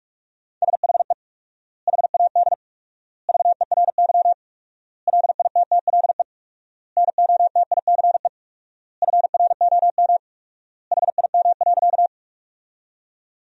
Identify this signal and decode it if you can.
Morse code, heard as a keyed tone, about 44 wpm, 700 Hz. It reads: SHE HAD VERY LITTLE NOTICE FROM HIM.
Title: 4x Word-Spacing - Sentences from Top 400 Words - 45wpm